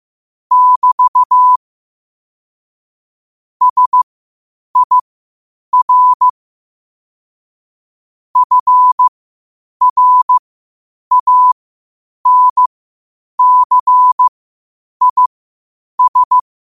Morse code, heard as a keyed tone, about 15 wpm, spaced as for 8 wpm, 1000 Hz. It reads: <BT> SIR FRANCIS
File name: qotdfiles / 8wpm